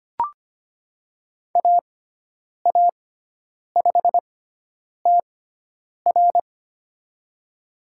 Morse code, heard as a keyed tone, about 25 wpm, spaced as for 9 wpm, 700 Hz. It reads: AA5TR
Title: USA Call Signs - 5 Characters - 6x Character Spacing - 25wpm